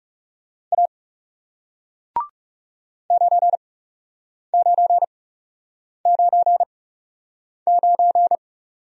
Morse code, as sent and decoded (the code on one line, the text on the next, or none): .- / ----. ----. ----. ----.
A 9999